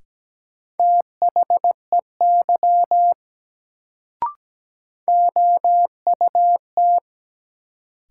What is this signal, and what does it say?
Morse code, heard as a keyed tone, about 17 wpm, 700 Hz. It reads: THEY OUT
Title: Sentences from Top 100 Words - New Word Drill - 17wpm